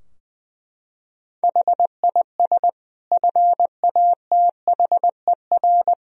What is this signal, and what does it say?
Morse code, heard as a keyed tone, about 20 wpm, 700 Hz. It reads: HIS FATHER
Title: Sentences from Top 1000 Words - Review Part 4 - 20wpm